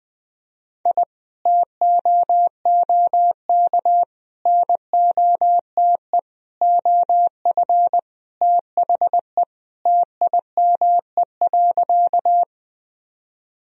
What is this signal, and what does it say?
Morse code, heard as a keyed tone, about 20 wpm, 700 Hz. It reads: I TOOK NOTE OF THE TIME.
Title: Sentences from Top 1000 Words - Review Part 2 - 20wpm